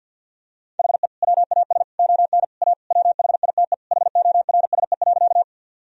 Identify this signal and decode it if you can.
Morse code, heard as a keyed tone, about 50 wpm, 700 Hz. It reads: HE WAS ON A WHITE HORSE.
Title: Sentences from Top 1000 Words - Review Part 1 - 50wpm